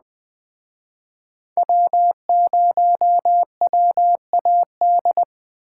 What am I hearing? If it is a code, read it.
Morse code, heard as a keyed tone, about 20 wpm, 700 Hz. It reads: W0WAD
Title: Speed Racing - USA Call Signs 17wpm